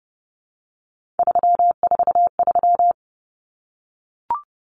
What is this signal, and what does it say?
Morse code, heard as a keyed tone, about 30 wpm, 700 Hz. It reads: 343
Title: Common Cut Numbers - 3-Digit 30wpm